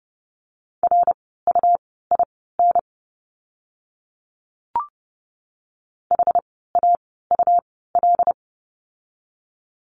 Morse code, heard as a keyed tone, about 30 wpm, spaced as for 17 wpm, 700 Hz. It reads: RUIN HAUL
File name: Lesson 17f - Letter u - Words 30wpm